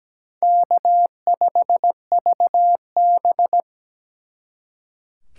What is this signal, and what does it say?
Morse code, heard as a keyed tone, about 20 wpm, 700 Hz. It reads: K5VB
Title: Speed Racing - USA 1x2 Call Signs 17wpm